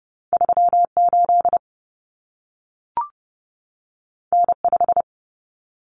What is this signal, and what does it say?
Morse code, heard as a keyed tone, about 30 wpm, 700 Hz. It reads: 38 N5
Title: Common Cut Numbers - 2-Digit 30wpm